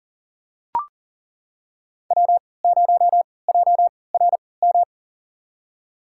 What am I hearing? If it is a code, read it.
Morse code, heard as a keyed tone, about 40 wpm, spaced as for 23 wpm, 700 Hz. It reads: W0JRM
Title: USA Call Signs - 5 Characters - 3x Character Spacing - 40wpm